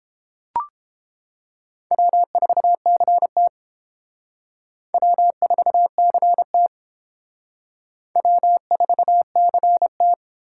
Morse code, heard as a keyed tone, about 28 wpm, 700 Hz. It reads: W4CT W4CT W4CT